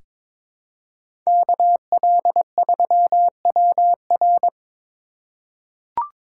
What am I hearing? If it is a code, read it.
Morse code, heard as a keyed tone, about 22 wpm, 700 Hz. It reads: KL3WR
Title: USA Call Signs - 5 Characters - 3x Character Spacing - 22wpm